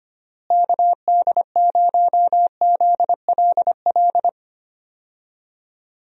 Morse code, heard as a keyed tone, about 25 wpm, 700 Hz. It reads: KD0ZLL
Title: USA Call Signs - 6 Characters - Repeated 2x - 25wpm